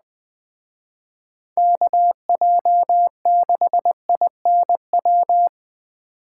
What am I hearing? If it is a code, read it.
Morse code, heard as a keyed tone, about 20 wpm, 700 Hz. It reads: KJ6INW